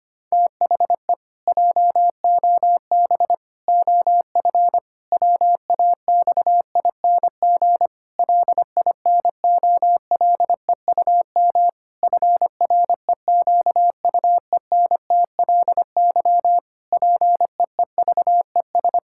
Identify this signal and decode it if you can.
Morse code, heard as a keyed tone, about 25 wpm, 700 Hz. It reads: THE JOB OF WAXING LINOLEUM FREQUENTLY PEEVES
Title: Pangram Challenge v2 - 25wpm